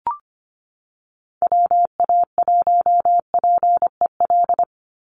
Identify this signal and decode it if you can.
Morse code, heard as a keyed tone, about 25 wpm, 700 Hz. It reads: WA1PEL
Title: USA Call Signs 25wpm